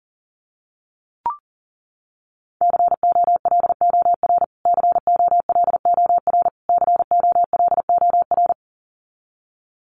Morse code, heard as a keyed tone, about 40 wpm, 700 Hz. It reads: COLOR COLOR COLOR
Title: Top 500 Words Repeated 40wpm